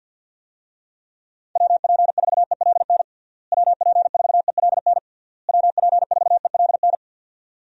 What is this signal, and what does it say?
Morse code, heard as a keyed tone, about 48 wpm, 700 Hz. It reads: WP4ELN WP4ELN WP4ELN
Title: USA Call Signs - 6 Characters - Repeated 3x - 50wpm